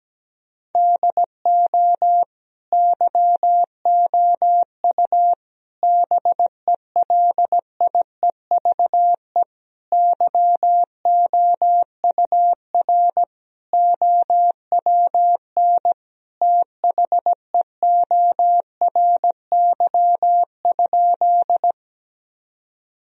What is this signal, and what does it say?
Morse code, heard as a keyed tone, about 17 wpm, 700 Hz. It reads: DO YOU BELIEVE YOUR OWN THEORY?
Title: Sentences from Top 1000 Words - Review Part 3 - 17wpm